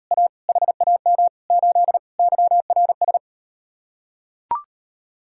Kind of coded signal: Morse code

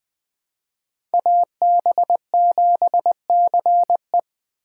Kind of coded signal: Morse code

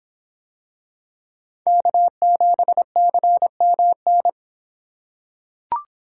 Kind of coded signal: Morse code

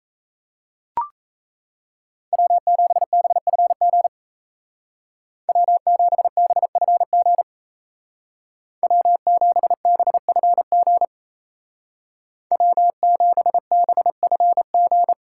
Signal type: Morse code

